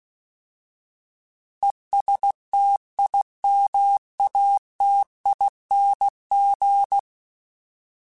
Morse code, heard as a keyed tone, about 16 wpm, 800 Hz. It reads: ESTIMATING
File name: Alphabetical / Alphabetical mp3s / Es @15